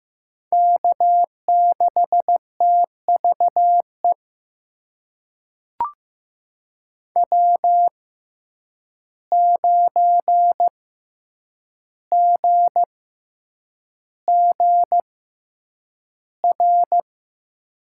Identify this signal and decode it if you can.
Morse code, heard as a keyed tone, about 15 wpm, spaced as for 8 wpm, 700 Hz. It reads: K6TVE W9GGR
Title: USA Call Signs - 5 Characters - 6x Character Spacing - 15wpm